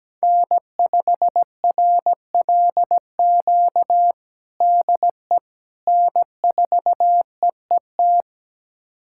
Morse code, heard as a keyed tone, about 17 wpm, 700 Hz. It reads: N5RLQ DE N4EET
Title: USA Callsign DE Callsign - 5 Characters - 17wpm